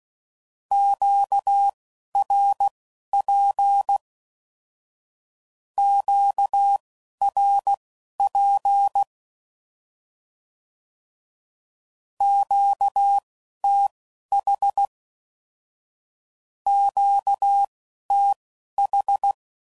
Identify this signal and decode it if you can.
Morse code, heard as a keyed tone, about 16 wpm, spaced as for 11 wpm, 800 Hz. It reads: QRP QRP QTH QTH
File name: CW Code Groups @10WPM / Q Calls / Q Calls Twice